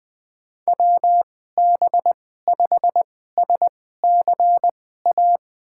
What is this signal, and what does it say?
Morse code, heard as a keyed tone, about 20 wpm, spaced as for 14 wpm, 700 Hz. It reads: WB5SCA
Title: USA Call Signs - 6 Characters - 2x Character Spacing - 20wpm